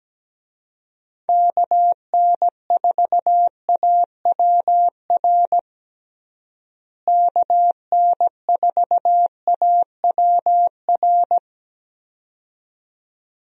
Morse code, heard as a keyed tone, about 17 wpm, 700 Hz. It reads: KN4AWR KN4AWR
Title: USA Call Signs - 6 Characters - Repeated 2x - 17wpm